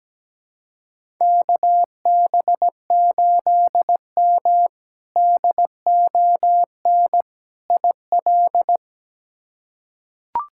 Morse code, heard as a keyed tone, about 17 wpm, 700 Hz. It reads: KB8M DON IL
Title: K1USN SST QSO Practice - 17wpm